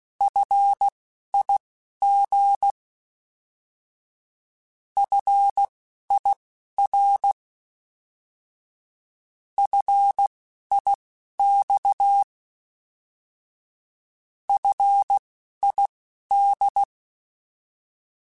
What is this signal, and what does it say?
Morse code, heard as a keyed tone, about 16 wpm, spaced as for 11 wpm, 800 Hz. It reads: FIG FIR FIX FID
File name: Alphabetical / Alphabetical mp3s / Fs @10